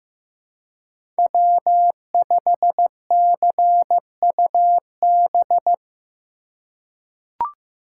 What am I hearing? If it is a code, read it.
Morse code, heard as a keyed tone, about 15 wpm, 700 Hz. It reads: W5CUB
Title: USA Call Signs - 5 Characters - 2x Character Spacing - 15wpm